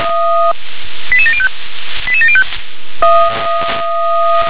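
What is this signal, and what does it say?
Five-tone selective call (ZVEI), heard as quick runs of five tones, tones of about 70 ms. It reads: 88705 88705
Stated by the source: ZVEI five-tone selective call 88765 received by SDR.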